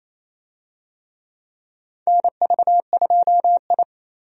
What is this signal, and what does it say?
Morse code, heard as a keyed tone, about 28 wpm, 700 Hz. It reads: NV2I
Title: USA Call Signs - 4 Characters - 28wpm